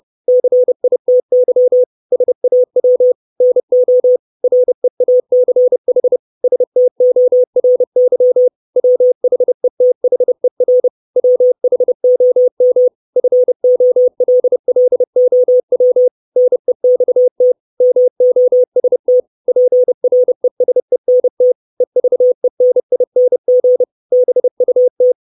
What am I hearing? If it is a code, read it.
Morse code, heard as a keyed tone, about 30 wpm, 500 Hz. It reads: CITY SAW NO REACH STORY WHETHER WHOM FOLLOW NEXT MOST PRESENT EVENING BUT
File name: 632 Words 30WPM